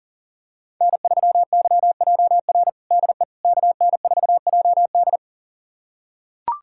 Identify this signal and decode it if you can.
Morse code, heard as a keyed tone, about 37 wpm, 700 Hz. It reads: N3YJR DE KN4JD
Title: USA Callsign DE Callsign - 5 Characters - 40wpm